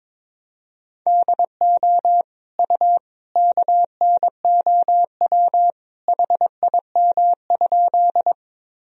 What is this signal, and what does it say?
Morse code, heard as a keyed tone, about 22 wpm, 700 Hz. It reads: DO U KNOW HIM?
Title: Sentences from Top 100 Words and Abbreviations 22wpm